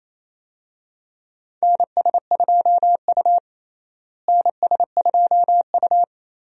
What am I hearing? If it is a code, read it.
Morse code, heard as a keyed tone, about 28 wpm, 700 Hz. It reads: NS2U NS2U
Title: USA Call Signs - 4 Characters - Repeated 2x - 28wpm